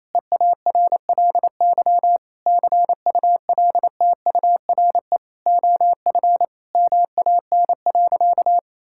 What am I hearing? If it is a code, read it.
Morse code, heard as a keyed tone, about 28 wpm, 700 Hz. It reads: EARLY CULTURE OF MAN.
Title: Sentences from Top 1000 Words - Review Part 6 - 28wpm